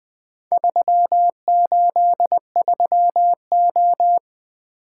Morse code, heard as a keyed tone, about 20 wpm, 700 Hz. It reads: 383O
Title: Four Letter-Number Combinations - 20wpm